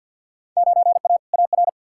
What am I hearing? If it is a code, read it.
Morse code, heard as a keyed tone, about 46 wpm, 700 Hz. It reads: 9A AR
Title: Field Day Exchanges Encore - 50wpm